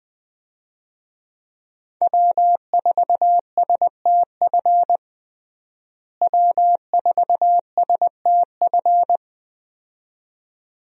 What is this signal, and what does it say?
Morse code, heard as a keyed tone, about 20 wpm, 700 Hz. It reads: W4STF W4STF
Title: USA Call Signs - 5 Characters - Repeated 2x - 20wpm